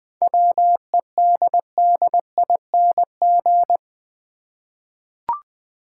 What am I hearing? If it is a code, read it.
Morse code, heard as a keyed tone, about 20 wpm, 700 Hz. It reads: WEDDING
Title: 7 Letter Words - First 3 Letters Given - 20wpm